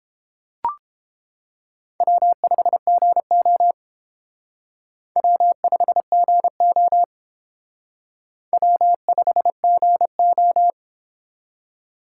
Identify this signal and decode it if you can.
Morse code, heard as a keyed tone, about 31 wpm, 700 Hz. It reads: W5GO W5GO W5GO